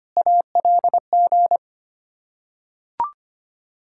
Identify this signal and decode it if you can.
Morse code, heard as a keyed tone, about 25 wpm, 700 Hz. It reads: ALG
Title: Ontario QSO Party Exchange 25wpm